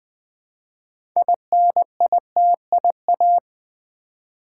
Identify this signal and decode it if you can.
Morse code, heard as a keyed tone, about 20 wpm, 700 Hz. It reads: INITIA
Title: Build A Word - 2 Letters At A Time - 8-Letter Words - 20wpm